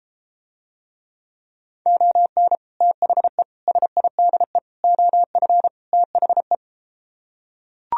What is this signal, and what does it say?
Morse code, heard as a keyed tone, about 33 wpm, 700 Hz. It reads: ON THE SIDE OF THE